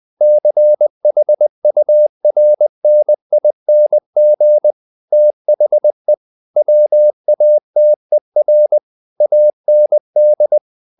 Morse code, heard as a keyed tone, about 20 wpm, 600 Hz. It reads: CHURNING THE WATER AND
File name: the old man and the sea / the old man and the sea part 1 20wpm